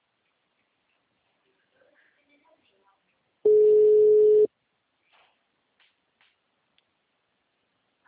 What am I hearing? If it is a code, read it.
Ringback tone (Europe, 425 Hz), with speech.